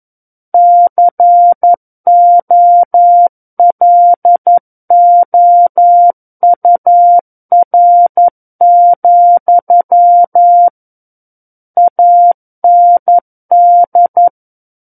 Morse code, heard as a keyed tone, about 11 wpm, 700 Hz. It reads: COLOUR, AND
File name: 02-Chapter 2 - 11 WPM